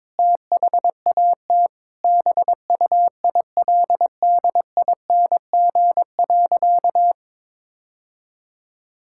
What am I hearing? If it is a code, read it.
Morse code, heard as a keyed tone, about 22 wpm, 700 Hz. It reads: THAT BUILDING.
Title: Sentences from Top 1000 Words - Review Part 5 - 22wpm